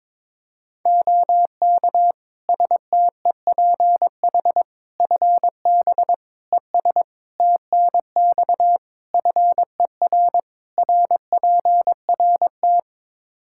Morse code, heard as a keyed tone, about 22 wpm, 700 Hz. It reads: OK STEPH FB ES TNX FER RPRT
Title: Practice QSO - Exchange 2 - Info - 22wpm